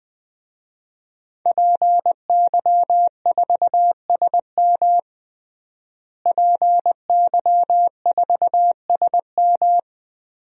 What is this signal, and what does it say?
Morse code, heard as a keyed tone, about 20 wpm, 700 Hz. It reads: PY4SM PY4SM
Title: DXCC-100 Call Signs - 5 Characters - Repeated 3x - 20wpm